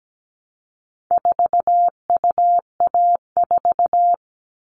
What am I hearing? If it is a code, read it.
Morse code, heard as a keyed tone, about 17 wpm, 700 Hz. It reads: 4UA4
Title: All Cut Numbers - 4-Digit 17wpm